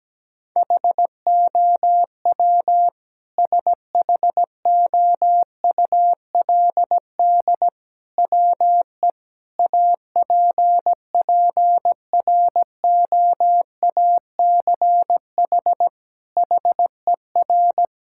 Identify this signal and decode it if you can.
Morse code, heard as a keyed tone, about 17 wpm, 700 Hz. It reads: HOW SHOULD WE APPROACH HER
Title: Sentences from Top 1000 Words - Review Part 6 - 17wpm